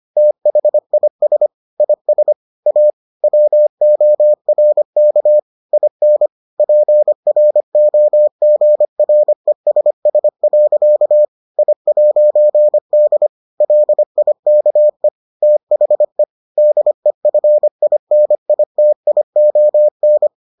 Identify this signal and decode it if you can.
Morse code, heard as a keyed tone, about 25 wpm, 600 Hz. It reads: THIS IS A WORK IN PROGRESS. I'D LIKE THE DEFINITION